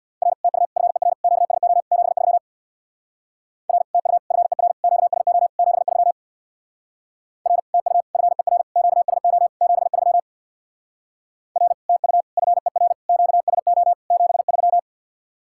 Morse code, heard as a keyed tone, about 61 wpm, 690 Hz. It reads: R TU FER QSO 73 R TU FER QSO 73 R TU FER QSO 73 R TU FER QSO 73
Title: Speed-Racing Common QSO Phrases - 50wpm